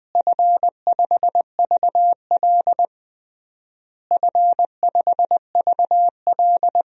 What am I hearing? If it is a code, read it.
Morse code, heard as a keyed tone, about 20 wpm, 700 Hz. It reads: F5VL F5VL
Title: DXCC-100 Call Signs - 4 Characters - Repeated 3x - 20wpm